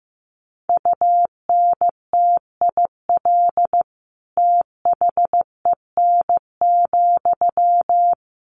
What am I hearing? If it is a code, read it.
Morse code, heard as a keyed tone, about 14 wpm, 700 Hz. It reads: UNTIL THEN,
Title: Sentences from Top 1500 Words 15wpm